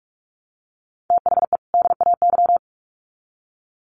Morse code, heard as a keyed tone, about 44 wpm, 700 Hz. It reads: THE DAY
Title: Sets of 2 Words 45wpm